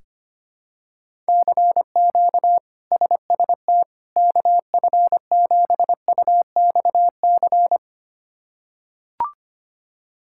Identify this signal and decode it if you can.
Morse code, heard as a keyed tone, about 25 wpm, 700 Hz. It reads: CQ SST KF7UXC
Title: K1USN SST QSO Practice - 25wpm